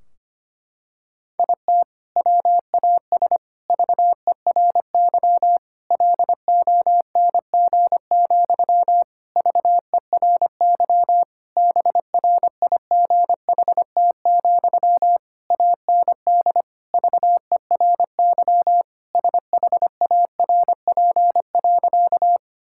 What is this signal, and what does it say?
Morse code, heard as a keyed tone, about 25 wpm, 700 Hz. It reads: IT WAS VERY LONG, VERY BRIGHT, AND VERY SHARP.
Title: Sentences from Top 1600 Words 25wpm